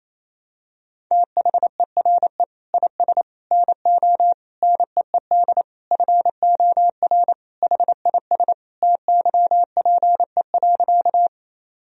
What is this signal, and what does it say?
Morse code, heard as a keyed tone, about 28 wpm, 700 Hz. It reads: THERE IS NO NEED FOR HIS TYPE.
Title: Sentences from Top 1000 Words - Review Part 3 - 28wpm